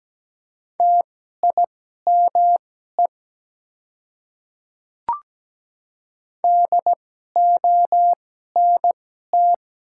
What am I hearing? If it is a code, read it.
Morse code, heard as a keyed tone, about 17 wpm, spaced as for 12 wpm, 700 Hz. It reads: TIME DONT
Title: 2x Character Spacing - Top 300 Words - 17wpm